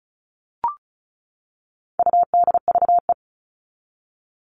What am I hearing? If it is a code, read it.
Morse code, heard as a keyed tone, about 34 wpm, 700 Hz. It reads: UDVE
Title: All Cut Numbers - 4-Digit 35wpm